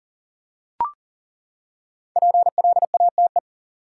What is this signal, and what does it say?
Morse code, heard as a keyed tone, about 40 wpm, 700 Hz. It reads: PLATE
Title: Five Letter Words - 40wpm